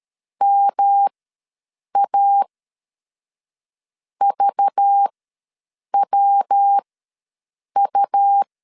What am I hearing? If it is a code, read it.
Morse code, heard as a keyed tone, about 13 wpm, spaced as for 6 wpm, 800 Hz. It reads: MAVWU
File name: LEZIONI / 6 PKX0 / 6 PK ws 12wpm 3116